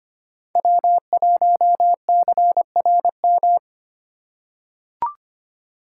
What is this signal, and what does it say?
Morse code, heard as a keyed tone, about 25 wpm, 700 Hz. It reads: W1CRM
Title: USA Call Signs - 3x Character Spacing - 25wpm